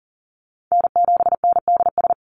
Morse code, heard as a keyed tone, about 40 wpm, 690 Hz. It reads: N7NDS
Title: USA Call Signs 40wpm